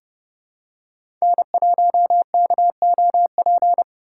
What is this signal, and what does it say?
Morse code, heard as a keyed tone, about 30 wpm, 700 Hz. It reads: N1KOP
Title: USA Call Signs - 2x Character Spacing - 30wpm